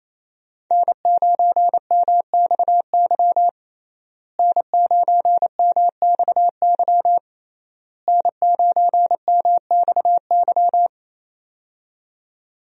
Morse code, heard as a keyed tone, about 28 wpm, 700 Hz. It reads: N9MXY N9MXY N9MXY
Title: USA Call Signs - 5 Characters - Repeated 3x - 28wpm